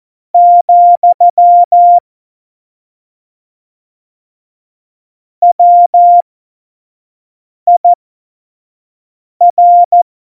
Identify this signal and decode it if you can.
Morse code, heard as a keyed tone, about 14 wpm, spaced as for 5 wpm, 700 Hz. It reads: , WIR
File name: Morsekurs / Witze / Witz 16. Oktober 2018, 25 F70